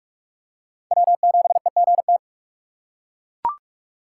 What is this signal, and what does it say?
Morse code, heard as a keyed tone, about 41 wpm, 700 Hz. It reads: W7EGT